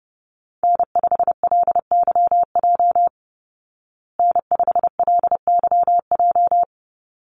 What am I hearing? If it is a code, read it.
Morse code, heard as a keyed tone, about 30 wpm, 700 Hz. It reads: N5LYJ N5LYJ
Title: Call Signs Repeated 3x - 30wpm